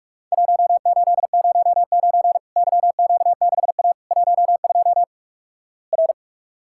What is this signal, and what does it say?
Morse code, heard as a keyed tone, about 48 wpm, 700 Hz. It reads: 1809 YQ6A 12 R
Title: Worked All Europe DX Contest - QTC Exchange - 45wpm